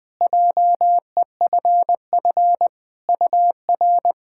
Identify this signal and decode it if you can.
Morse code, heard as a keyed tone, about 20 wpm, 700 Hz. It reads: JEFF UR
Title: POTA QSO Practice - 20wpm